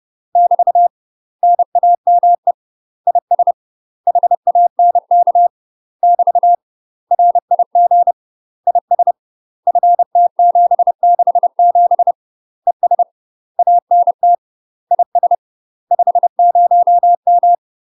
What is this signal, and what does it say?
Morse code, heard as a keyed tone, about 30 wpm, 700 Hz. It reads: <BT> NAME IS HANK <BT> RIG IS FT767 ES ANT IS 50M
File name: UK MASTER QSOs TRAINING / 30WPM / UK MASTER QSOs 0004